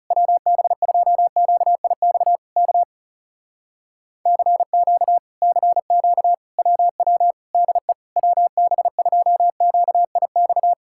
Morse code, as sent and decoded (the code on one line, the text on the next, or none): .-- -... ..--- --.- .. -..- / -.- / -.-. --.- / -.-. --.- / .-- .-- / -.. / .-- -... ..--- --.- .. -..-
WB2QIX K CQ CQ WW DE WB2QIX